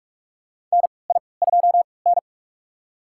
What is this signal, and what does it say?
Morse code, heard as a keyed tone, about 46 wpm, spaced as for 26 wpm, 700 Hz. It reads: NI2N
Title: USA Call Signs - 4 Characters - 3x Character Spacing - 45wpm